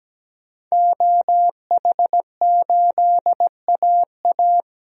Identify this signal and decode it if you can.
Morse code, heard as a keyed tone, about 17 wpm, 700 Hz. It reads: OH8AA
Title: DXCC-100 Call Signs - 5 Characters - 17wpm